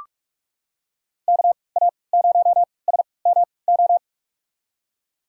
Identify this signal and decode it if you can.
Morse code, heard as a keyed tone, about 42 wpm, spaced as for 25 wpm, 700 Hz. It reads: KA0SMO
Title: USA Call Signs - 6 Characters - 3x Character Spacing - 45wpm